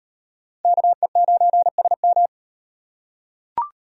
Morse code, heard as a keyed tone, about 38 wpm, 700 Hz. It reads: KE9SM